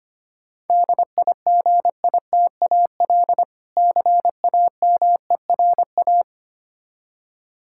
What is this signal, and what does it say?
Morse code, heard as a keyed tone, about 25 wpm, 700 Hz. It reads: DIGITAL CAMERA
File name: practice / 25wpm / Difficult Words with Hints - 25wpm